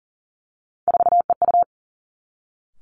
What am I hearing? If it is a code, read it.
Morse code, heard as a keyed tone, about 38 wpm, 700 Hz. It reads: 4EU
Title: Three Challenging Letter-Number Combinations 40wpm